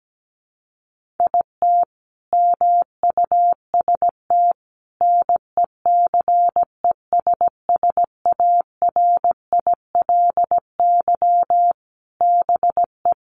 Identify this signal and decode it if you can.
Morse code, heard as a keyed tone, about 17 wpm, 700 Hz. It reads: IT MUST NECESSARILY BE
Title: Sentences from Top 1400 Words 17wpm